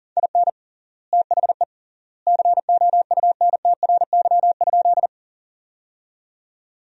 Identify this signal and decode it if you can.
Morse code, heard as a keyed tone, about 37 wpm, 700 Hz. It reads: IN THE COUNTRY?
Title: 3x Word-Spacing - Sentences from Top 300 Words - 40wpm